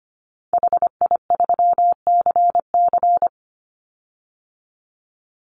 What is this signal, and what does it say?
Morse code, heard as a keyed tone, about 25 wpm, 700 Hz. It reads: HI3CC
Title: Call Signs - 25wpm